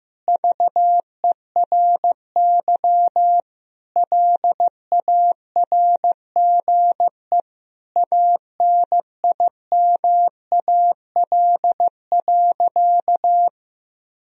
Morse code, heard as a keyed tone, about 16 wpm, 700 Hz. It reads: VERY LARGE ANIMAL.
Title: Sentences from Top 1000 Words - Review Part 1 - 15wpm